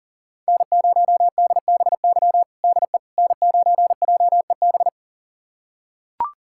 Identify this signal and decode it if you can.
Morse code, heard as a keyed tone, about 37 wpm, 700 Hz. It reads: N0DBY DE N9JEB
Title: USA Callsign DE Callsign 40wpm